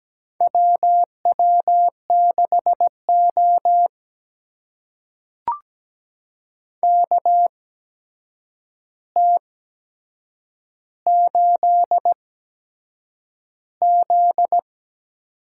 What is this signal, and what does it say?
Morse code, heard as a keyed tone, about 17 wpm, spaced as for 7 wpm, 700 Hz. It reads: WW6O KT8Z
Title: USA Call Signs - 4 Characters - 8x Character Spacing - 17wpm